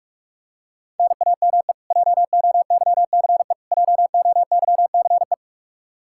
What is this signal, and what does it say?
Morse code, heard as a keyed tone, about 46 wpm, 700 Hz. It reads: NAME JOYCE JOYCE
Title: Practice QSO - Exchange 1 - Beginning - 45wpm